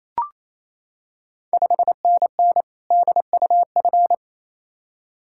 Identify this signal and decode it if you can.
Morse code, heard as a keyed tone, about 27 wpm, 690 Hz. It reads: 5NN DUF
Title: Ontario QSO Party Exchange 28wpm